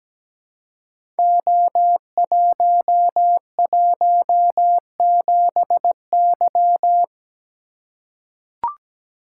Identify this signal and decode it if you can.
Morse code, heard as a keyed tone, about 17 wpm, 700 Hz. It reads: O117Y
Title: Code Groups - 5 Letter:Numbers - 17wpm